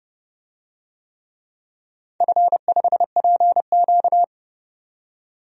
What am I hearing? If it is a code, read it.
Morse code, heard as a keyed tone, about 30 wpm, 700 Hz. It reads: F5PQ
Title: DXCC-100 Call Signs - 4 Characters - 30wpm